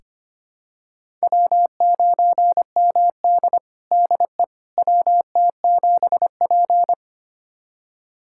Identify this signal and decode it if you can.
Morse code, heard as a keyed tone, about 25 wpm, 700 Hz. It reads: W9MD DE WT7P
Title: USA Callsign DE Callsign - 4 Characters - 25wpm